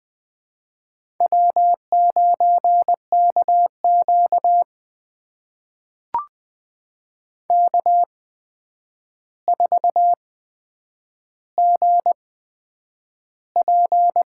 Morse code, heard as a keyed tone, about 20 wpm, spaced as for 9 wpm, 700 Hz. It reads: W9KQ K4GP
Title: USA Call Signs - 4 Characters - 8x Character Spacing - 20wpm